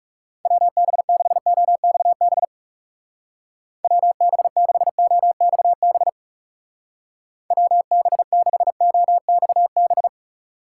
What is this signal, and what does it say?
Morse code, heard as a keyed tone, about 36 wpm, 700 Hz. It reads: WB6OXB WB6OXB WB6OXB
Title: Speed Racing - USA 2x3 Call Signs 30wpm